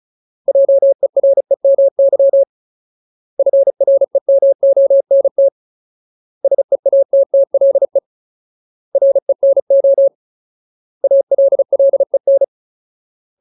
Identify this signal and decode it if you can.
Morse code, heard as a keyed tone, about 36 wpm, 550 Hz. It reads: JEREMY FREMONT SEATTLE RENO ALLEN
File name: Ultimate Word List / 35WPM Ultimate